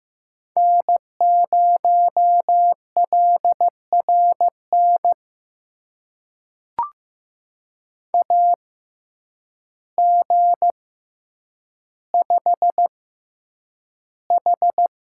Morse code, heard as a keyed tone, about 15 wpm, spaced as for 8 wpm, 700 Hz. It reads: N0LRN AG5H
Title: USA Call Signs - 6x Character Spacing - 15wpm